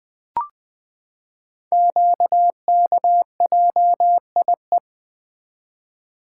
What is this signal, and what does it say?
Morse code, heard as a keyed tone, about 20 wpm, 700 Hz. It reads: QKJIE
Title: Code Groups - 5 Letters - 20wpm